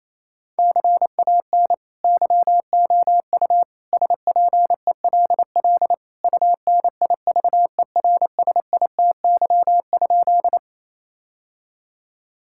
Morse code, heard as a keyed tone, about 28 wpm, 700 Hz. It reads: CAN YOU SPELL UNIVERSITY?
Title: Sentences from Top 1000 Words - Review Part 5 - 28wpm